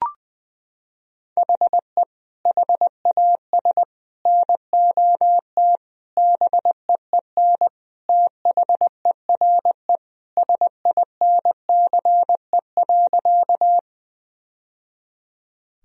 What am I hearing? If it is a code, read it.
Morse code, heard as a keyed tone, about 21 wpm, 700 Hz. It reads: HE HAS NOT BEEN THERE SINCE.
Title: Sentences from Top 1000 Words - Review Part 1 - 20wpm